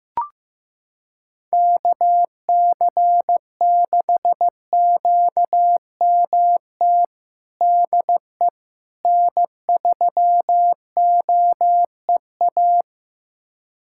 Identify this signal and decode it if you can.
Morse code, heard as a keyed tone, about 15 wpm, 700 Hz. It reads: KC6QMT DE N3OEA
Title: USA Callsign DE Callsign 15wpm